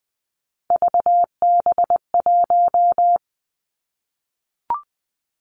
Morse code, heard as a keyed tone, about 20 wpm, 700 Hz. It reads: VB1
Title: Three Challenging Letter-Number Combinations 20wpm